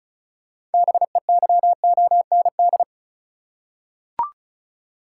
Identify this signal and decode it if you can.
Morse code, heard as a keyed tone, about 36 wpm, 700 Hz. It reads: BEYOND
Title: Six Letter Words - 35wpm